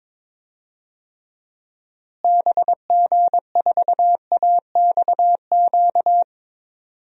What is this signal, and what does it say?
Morse code, heard as a keyed tone, about 22 wpm, 700 Hz. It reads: BG4AXQ
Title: DXCC-100 Call Signs - 6 Characters - Repeated 2x - 22wpm